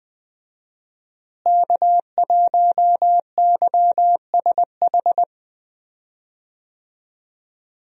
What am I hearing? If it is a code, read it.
Morse code, heard as a keyed tone, about 20 wpm, 700 Hz. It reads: K1YSH
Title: USA Call Signs - 5 Characters - Repeated 2x - 20wpm